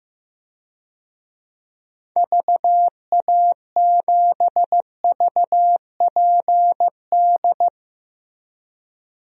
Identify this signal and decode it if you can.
Morse code, heard as a keyed tone, about 16 wpm, 700 Hz. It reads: VA7VPD
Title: DXCC-100 Call Signs - 6 Characters - Repeated 3x - 15wpm